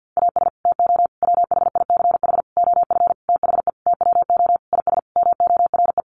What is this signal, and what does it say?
Morse code, heard as a keyed tone, about 50 wpm, 700 Hz. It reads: AS TO WHICH OF THE TWO IS MORE
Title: Sentences from Top 1200 Words 50wpm